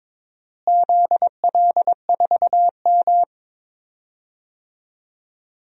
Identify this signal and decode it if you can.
Morse code, heard as a keyed tone, about 21 wpm, 700 Hz. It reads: ZL4M
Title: DXCC-100 Call Signs - 4 Characters - Repeated 2x - 22wpm